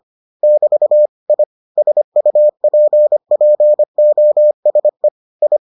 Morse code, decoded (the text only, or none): <BT> I SUPPOSE I